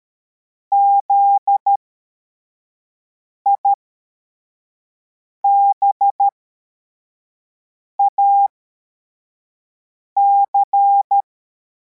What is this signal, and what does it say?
Morse code, heard as a keyed tone, about 13 wpm, spaced as for 4 wpm, 800 Hz. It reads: ZIBAC
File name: formation / cw / Text61